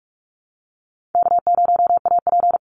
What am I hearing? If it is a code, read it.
Morse code, heard as a keyed tone, about 44 wpm, 700 Hz. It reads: K0AP
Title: Easy Call Signs Repeated 2x - 45wpm